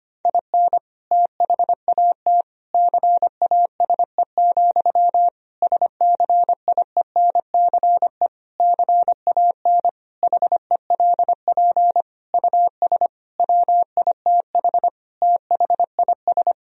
Morse code, heard as a keyed tone, about 25 wpm, 700 Hz. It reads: IN THAT CASE, SCIENCE CAN HELP US WITH THIS
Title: Sentences from Top 1000 Words - Review Part 2 - 25wpm